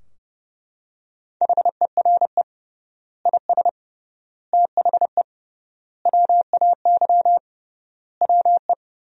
Morse code, heard as a keyed tone, about 30 wpm, 700 Hz. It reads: HERE IS THE WAY WE
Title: 3x Word-Spacing - Sentences from Top 200 Words - 30wpm